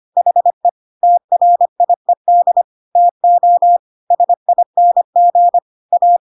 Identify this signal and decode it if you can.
Morse code, heard as a keyed tone, about 25 wpm, 700 Hz. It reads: HE TRIED TO SING A
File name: Pinocchio / Pinocchio ch1 25 25 ik1hge